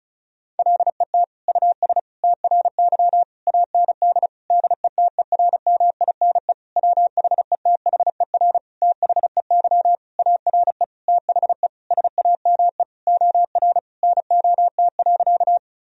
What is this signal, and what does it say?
Morse code, heard as a keyed tone, about 36 wpm, 700 Hz. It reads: LET US TRY AND DETERMINE WHETHER THEY ARE THE SAME OR NOT.
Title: Sentences from Top 1000 Words - Review Part 4 - 35wpm